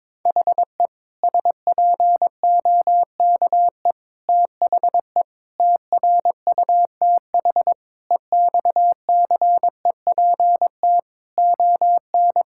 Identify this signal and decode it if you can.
Morse code, heard as a keyed tone, about 21 wpm, 700 Hz. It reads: HE SPOKE THE TRUTH EXCEPT ON
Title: Sentences from Top 1600 Words 22wpm